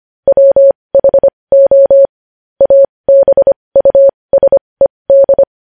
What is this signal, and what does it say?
Morse code, heard as a keyed tone, about 25 wpm, spaced as for 20 wpm, 550 Hz. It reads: WHO ABUSED
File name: podcast / News headlines-1x-25x20